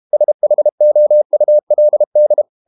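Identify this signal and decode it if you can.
Morse code, heard as a keyed tone, about 32 wpm, 600 Hz. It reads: SHOULD
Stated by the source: Morse code practice at 32 wpm from CW med Gnister, reading Ken Follett's novel Never in English.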